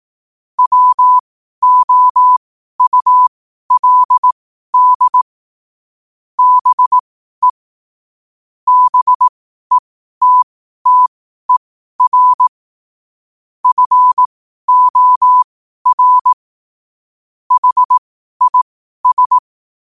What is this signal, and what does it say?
Morse code, heard as a keyed tone, about 18 wpm, spaced as for 12 wpm, 1000 Hz. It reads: WOULD BE BETTER FOR HIS
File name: qotdfiles / 12wpm